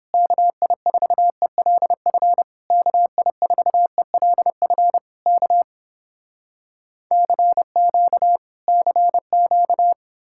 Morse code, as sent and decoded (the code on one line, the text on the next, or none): -.- .. ....- . .-.. ..-. / -.- .. ....- . .-.. ..-. / -.- / -.-. --.- / -.-. --.-
KI4ELF KI4ELF K CQ CQ